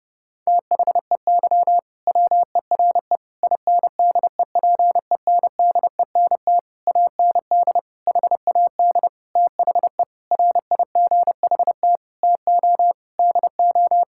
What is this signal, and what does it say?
Morse code, heard as a keyed tone, about 32 wpm, 700 Hz. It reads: THEY WERE INDEPENDENT AND HAD THE RIGHT TO DO
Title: Sentences from Top 1000 Words - Review Part 6 - 30wpm